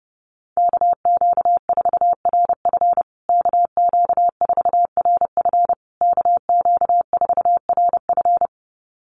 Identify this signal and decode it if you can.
Morse code, heard as a keyed tone, about 30 wpm, 700 Hz. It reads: KQ4RF KQ4RF KQ4RF
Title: USA Call Signs Repeated 3x 30wpm